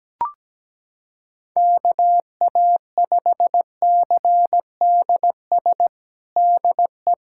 Morse code, heard as a keyed tone, about 17 wpm, 700 Hz. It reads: KA5CDS DE
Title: USA Callsign DE Callsign 17wpm